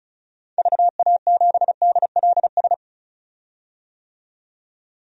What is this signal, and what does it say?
Morse code, heard as a keyed tone, about 36 wpm, 700 Hz. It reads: VA7DLS
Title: DXCC-100 Call Signs - 6 Characters - 35wpm